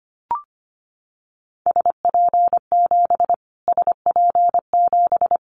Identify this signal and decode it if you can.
Morse code, heard as a keyed tone, about 25 wpm, 700 Hz. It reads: SP7 SP7
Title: Callsign Prefixes for Top 25 DXCC - Repeated 3x - 25wpm